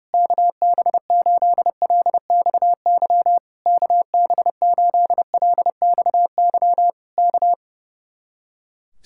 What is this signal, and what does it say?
Morse code, heard as a keyed tone, about 28 wpm, 700 Hz. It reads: KB8LXY KB8LXY K
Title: Speed-Racing Common QSO Phrases - 30wpm